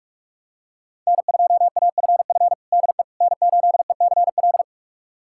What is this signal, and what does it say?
Morse code, heard as a keyed tone, about 41 wpm, 700 Hz. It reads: N2AFF DE N8ECL
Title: USA Callsign DE Callsign - 5 Characters - 45wpm